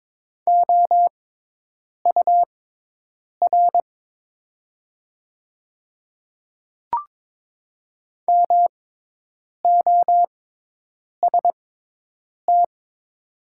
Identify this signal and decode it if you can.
Morse code, heard as a keyed tone, about 22 wpm, spaced as for 8 wpm, 690 Hz. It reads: OUR MOST
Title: 6x Character Spacing - Top 200 Words - 22wpm